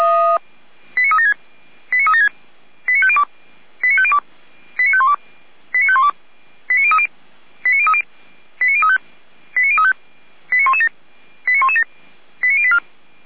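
Five-tone selective call (ZVEI), heard as quick runs of five tones, tones of about 70 ms. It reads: 89376 89376 89502 89502 89512 89512 89030 89030 89035 89035 89117 89117 89084
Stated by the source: ZVEI five-tone selective call received by SDR.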